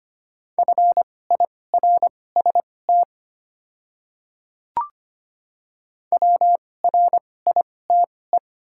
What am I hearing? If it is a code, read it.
Morse code, heard as a keyed tone, about 25 wpm, spaced as for 18 wpm, 700 Hz. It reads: FIRST WRITE
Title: 2x Character Spacing - Top 300 Words - 25wpm